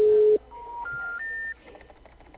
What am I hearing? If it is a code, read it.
Special information tone.
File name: out- -2019- - - .81756